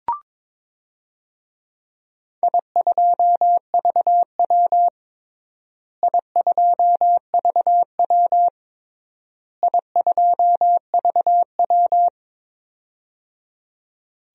Morse code, heard as a keyed tone, about 22 wpm, 700 Hz. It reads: I2VW I2VW I2VW
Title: DXCC-100 Call Signs - 4 Characters - Repeated 3x - 22wpm